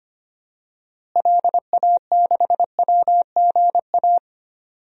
Morse code, heard as a keyed tone, about 25 wpm, 700 Hz. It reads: LA6WGA
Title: DXCC-100 Call Signs - 6 Characters - Repeated 2x - 25wpm